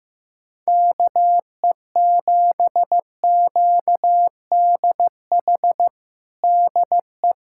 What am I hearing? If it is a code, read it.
Morse code, heard as a keyed tone, about 15 wpm, 700 Hz. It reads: KE7QDH DE
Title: USA Callsign DE Callsign - 6 Characters - 15wpm